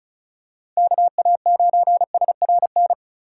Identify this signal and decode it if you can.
Morse code, heard as a keyed tone, about 34 wpm, 700 Hz. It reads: KA9SRN
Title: USA Call Signs - 6 Characters - 2x Character Spacing - 35wpm